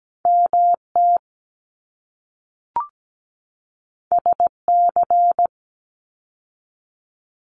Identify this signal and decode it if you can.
Morse code, heard as a keyed tone, about 17 wpm, 700 Hz. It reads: MT SC
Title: ARRL Sections 17wpm